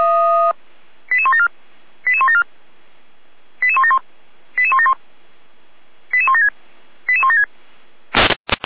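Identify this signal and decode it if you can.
Five-tone selective call (ZVEI), heard as quick runs of five tones, tones of about 70 ms. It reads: 88174 88174 88171 88171 88176 88176
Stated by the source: ZVEI five-tone selective call 88307 received by SDR, fire brigade feuerwehr-orlamuende.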